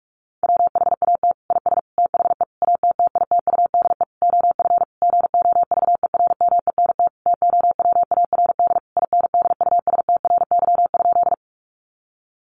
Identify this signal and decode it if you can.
Morse code, heard as a keyed tone, about 44 wpm, 700 Hz. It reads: WHAT IS THE ATTITUDE OF GOVERMENT TOWARD INDUSTRY?
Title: Sentences from Top 1300 Words 45wpm